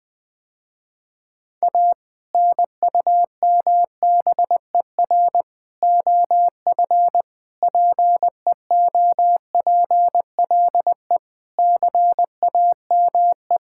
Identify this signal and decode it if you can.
Morse code, heard as a keyed tone, about 20 wpm, 700 Hz. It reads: A NUMBER OF PEOPLE CAME
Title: Sentences with 2-Word Phrases - Spoken Beginning - 20wpm